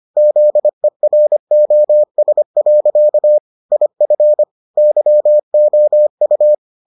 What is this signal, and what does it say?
Morse code, heard as a keyed tone, about 25 wpm, 600 Hz. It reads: ZEROS. IF YOU